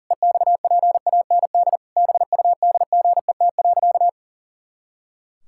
Morse code, heard as a keyed tone, about 40 wpm, 700 Hz. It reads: EXPAND BUDGET.
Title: Pangram Challenge v2 - 40wpm